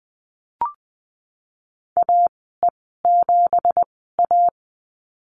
Morse code, heard as a keyed tone, about 20 wpm, spaced as for 15 wpm, 700 Hz. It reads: AE7A
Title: 173 Lesson 28g - Number 9 - Call Signs 20wpm